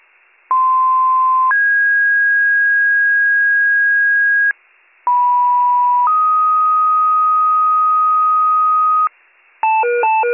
Two-tone page, heard as a one-second tone, then a three-second tone, about 1040 then 1650 Hz, about 980 then 1250 Hz, with noise. It is heard from 0.5 to 9.1 s.